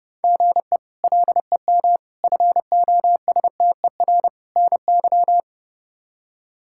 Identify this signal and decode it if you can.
Morse code, heard as a keyed tone, about 30 wpm, 700 Hz. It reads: GE LEM FOSTER NY
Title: K1USN SST QSO Practice - 30wpm